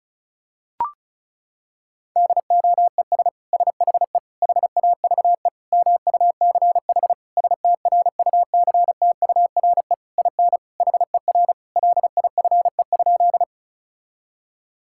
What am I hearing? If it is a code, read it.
Morse code, heard as a keyed tone, about 34 wpm, 700 Hz. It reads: DOES SHE HAVE MUCH STRUCTURE IN HER LIFE?
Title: Sentences from Top 1000 Words - Review Part 6 - 35wpm